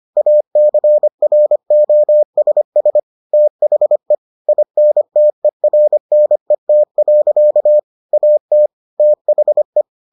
Morse code, heard as a keyed tone, about 25 wpm, 600 Hz. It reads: ACROSS THE INTERNET. AT THE